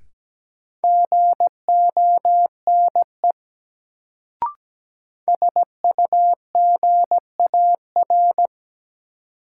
Morse code, heard as a keyed tone, about 17 wpm, 700 Hz. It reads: GONE SUGAR
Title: Sentences from Top 1200 Words - New Word Drill - 17wpm